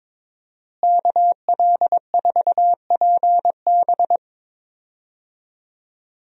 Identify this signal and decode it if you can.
Morse code, heard as a keyed tone, about 22 wpm, 700 Hz. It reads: KL4PB
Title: USA Call Signs - 5 Characters - Repeated 3x - 22wpm